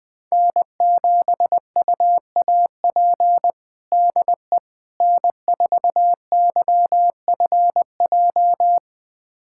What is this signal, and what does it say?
Morse code, heard as a keyed tone, about 20 wpm, 700 Hz. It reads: N7UAP DE N4YFJ
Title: USA Callsign DE Callsign 20wpm